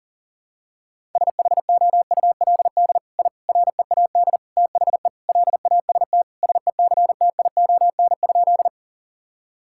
Morse code, heard as a keyed tone, about 40 wpm, 700 Hz. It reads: SHOULD I READ THE LAST SECTION?
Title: Sentences from Top 1000 Words - Review Part 5 - 40wpm